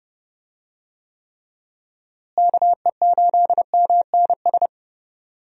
Morse code, heard as a keyed tone, about 30 wpm, 700 Hz. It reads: KE8MNS
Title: USA Call Signs - 6 Characters - Repeated 3x - 30wpm